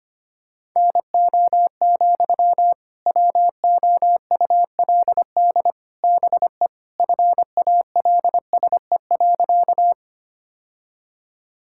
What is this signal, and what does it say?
Morse code, heard as a keyed tone, about 25 wpm, 700 Hz. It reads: NO, WOULD BE FALSE.
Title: Sentences from Top 1600 Words 25wpm